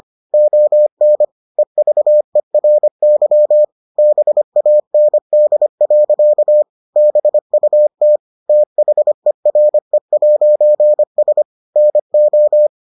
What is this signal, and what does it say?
Morse code, heard as a keyed tone, about 25 wpm, 600 Hz. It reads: ON EVERY BAND. BUT THERE'S NO